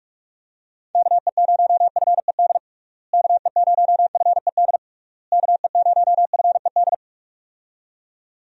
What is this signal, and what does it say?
Morse code, heard as a keyed tone, about 45 wpm, 700 Hz. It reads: KE0FED KE0FED KE0FED
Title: USA Call Signs - 6 Characters - Repeated 3x - 45wpm